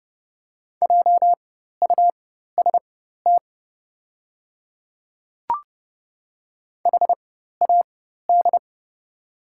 Morse code, heard as a keyed tone, about 30 wpm, spaced as for 14 wpm, 700 Hz. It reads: JUST HAD
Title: 4x Character Spacing - Top 200 Words - 30wpm